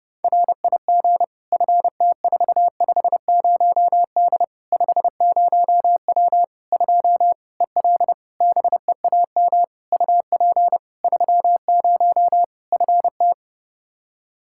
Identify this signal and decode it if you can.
Morse code, heard as a keyed tone, about 30 wpm, 700 Hz. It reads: RIG FT450D 50W 2 EL BEAM UP 30 FT
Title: Practice QSO - Exchange 2 - Info - 30wpm